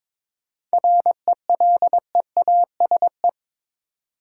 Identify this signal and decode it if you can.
Morse code, heard as a keyed tone, about 22 wpm, 700 Hz. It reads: RELEASE
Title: Seven Letter Words - 22wpm